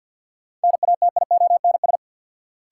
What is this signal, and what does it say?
Morse code, heard as a keyed tone, about 49 wpm, 700 Hz. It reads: NATIONS
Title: Sets of 3 Words - New Word Drill - 50wpm